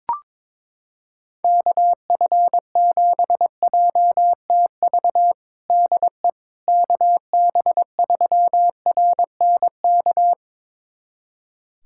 Morse code, heard as a keyed tone, about 22 wpm, 700 Hz. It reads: KF7JTV DE KB3RNK
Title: USA Callsign DE Callsign 22wpm